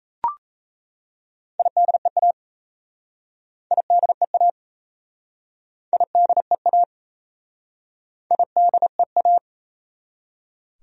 Morse code, heard as a keyed tone, about 37 wpm, 700 Hz. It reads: IDEA IDEA IDEA IDEA